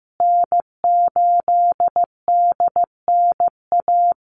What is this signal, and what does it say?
Morse code, heard as a keyed tone, about 15 wpm, 700 Hz. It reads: N8DNA
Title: Call Signs Repeated 3x - 15wpm